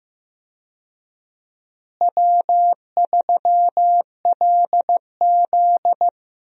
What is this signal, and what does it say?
Morse code, heard as a keyed tone, about 15 wpm, 700 Hz. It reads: W3LZ
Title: USA Call Signs - 4 Characters - Repeated 3x - 15wpm